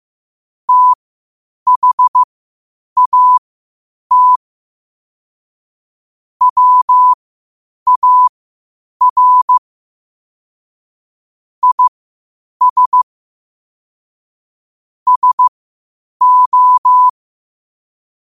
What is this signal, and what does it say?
Morse code, heard as a keyed tone, about 15 wpm, spaced as for 8 wpm, 1000 Hz. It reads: THAT WAR IS SO
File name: qotdfiles / 8wpm